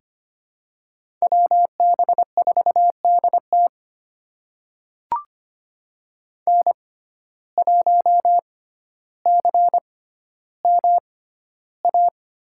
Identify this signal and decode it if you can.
Morse code, heard as a keyed tone, about 25 wpm, spaced as for 13 wpm, 700 Hz. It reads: WB4DT N1CMA
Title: USA Call Signs - 5 Characters - 6x Character Spacing - 25wpm